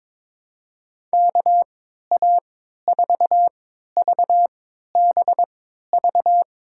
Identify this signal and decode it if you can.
Morse code, heard as a keyed tone, about 23 wpm, spaced as for 13 wpm, 700 Hz. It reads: KA4VBV
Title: USA Call Signs - 6 Characters - 3x Character Spacing - 22wpm